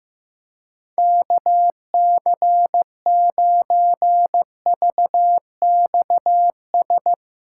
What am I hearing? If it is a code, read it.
Morse code, heard as a keyed tone, about 15 wpm, 700 Hz. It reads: KC9VXS
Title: USA Callsign DE Callsign - 6 Characters - 15wpm